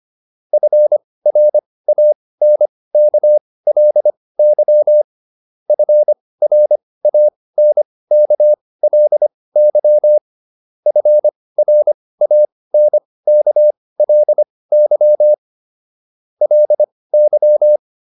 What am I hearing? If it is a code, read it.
Morse code, heard as a keyed tone, about 25 wpm, spaced as for 18 wpm, 600 Hz. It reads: FRANKLY FRANKLY FRANKLY LY